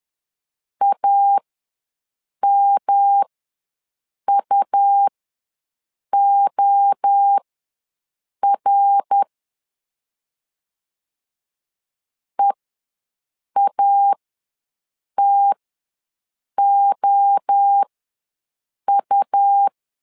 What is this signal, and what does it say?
Morse code, heard as a keyed tone, about 10 wpm, spaced as for 6 wpm, 800 Hz. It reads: AMUOR EATOU
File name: LEZIONI / 3 MWRO / 3 MWRO ws 10wpm 2310